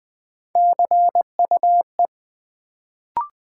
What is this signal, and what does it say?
Morse code, heard as a keyed tone, about 20 wpm, 700 Hz. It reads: CUE
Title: Build A Word - 1 Letter At A Time - 3-Letter Words - 20wpm